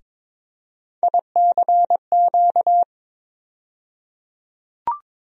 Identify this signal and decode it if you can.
Morse code, heard as a keyed tone, about 22 wpm, 700 Hz. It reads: ICQ
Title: USA Call Sign Postfixes - 3 Letters 22wpm